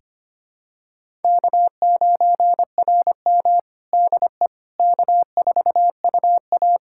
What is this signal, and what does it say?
Morse code, heard as a keyed tone, about 25 wpm, 700 Hz. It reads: K9RM DE K4UA
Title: USA Callsign DE Callsign - 4 Characters - 25wpm